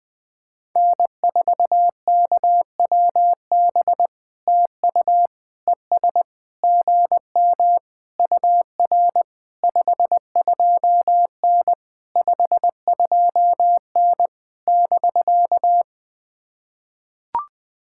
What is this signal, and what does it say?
Morse code, heard as a keyed tone, about 20 wpm, 700 Hz. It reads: N4KWB TU ES GM UR 52N 52N <BK>
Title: POTA QSO Practice - 20wpm